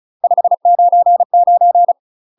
Morse code, heard as a keyed tone, about 35 wpm, 700 Hz. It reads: 599
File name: UK MASTER QSOs TRAINING / 35WPM / UK MASTER QSOs 0029